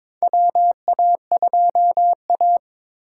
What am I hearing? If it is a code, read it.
Morse code, heard as a keyed tone, about 22 wpm, 700 Hz. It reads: WA2A
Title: USA Call Signs - 3x Character Spacing - 22wpm